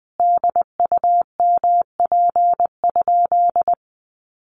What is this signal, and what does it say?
Morse code, heard as a keyed tone, about 20 wpm, 700 Hz. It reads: DUMP?
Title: Sentences from Top 1300 Words 20wpm